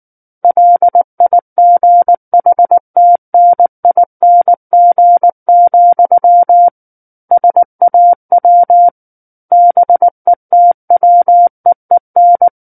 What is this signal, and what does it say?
Morse code, heard as a keyed tone, about 19 wpm, 700 Hz. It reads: LIGHTNING, SAW BETWEEN
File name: Book One / 10-Chapter 10 - 19 WPM